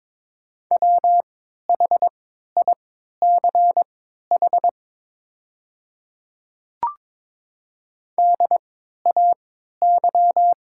Morse code, heard as a keyed tone, about 22 wpm, spaced as for 12 wpm, 700 Hz. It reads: WHICH DAY
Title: 3x Character Spacing - Top 100 Words - 22wpm